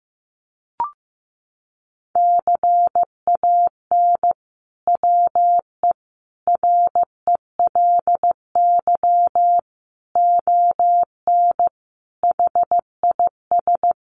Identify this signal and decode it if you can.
Morse code, heard as a keyed tone, about 15 wpm, 700 Hz. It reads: CAN WE RELY ON HIS
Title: Sentences from Top 1200 Words 15wpm